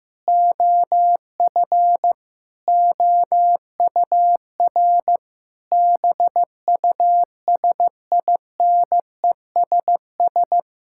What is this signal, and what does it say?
Morse code, heard as a keyed tone, about 15 wpm, 700 Hz. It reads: OF OUR BUSINESS
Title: Sentences from Top 1000 Words - Review Part 6 - 15wpm